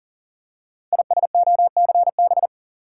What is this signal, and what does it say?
Morse code, heard as a keyed tone, about 44 wpm, 690 Hz. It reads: ISOCB
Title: Code Groups - 5 Letters - 40wpm